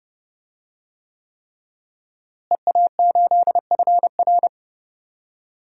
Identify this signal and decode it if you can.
Morse code, heard as a keyed tone, about 31 wpm, 700 Hz. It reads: EA8FR
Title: DXCC-100 Call Signs - 5 Characters - Repeated 3x - 30wpm